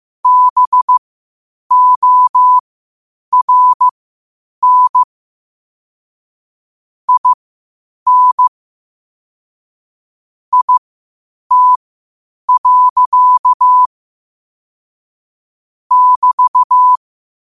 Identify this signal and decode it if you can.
Morse code, heard as a keyed tone, about 15 wpm, spaced as for 8 wpm, 1000 Hz. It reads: BORN IN IT. <BT>